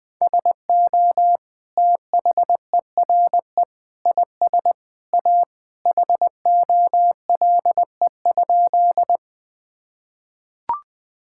Morse code, heard as a keyed tone, about 20 wpm, 700 Hz. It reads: SO THERE IS A HOLE?
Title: Sentences from Top 1600 Words 20wpm